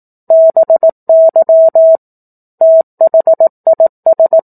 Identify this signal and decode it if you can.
Morse code, heard as a keyed tone, about 18 wpm, 650 Hz. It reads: BY THIS
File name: 18 WPM